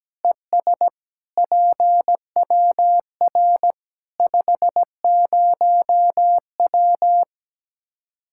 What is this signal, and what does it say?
Morse code, heard as a keyed tone, about 17 wpm, 700 Hz. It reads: ES PWR 50W
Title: Practice QSO - Exchange 2 - Info - 17wpm